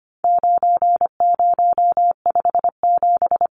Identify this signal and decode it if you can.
Morse code, heard as a keyed tone, about 25 wpm, 700 Hz. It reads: 9057
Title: Numbers - 4-Digit 25wpm